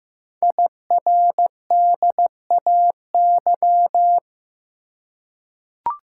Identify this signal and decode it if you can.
Morse code, heard as a keyed tone, about 15 wpm, 700 Hz. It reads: IRDAY
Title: Code Groups - 5 Letters - 15wpm